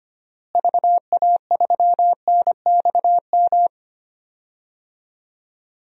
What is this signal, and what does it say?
Morse code, heard as a keyed tone, about 25 wpm, 700 Hz. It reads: VA3NXM
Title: DXCC-100 Call Signs - 6 Characters - Repeated 2x - 25wpm